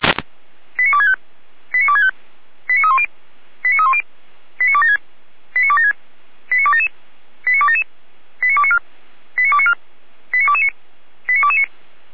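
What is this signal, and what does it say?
Five-tone selective call (ZVEI), heard as quick runs of five tones, tones of about 70 ms. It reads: 89265 89265 89310 89310 89276 89276 89277 89277 89294 89294 89229 89229
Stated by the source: ZVEI five-tone call received by SDR, from the Feuerwehr Orlamünde fire brigade audio server.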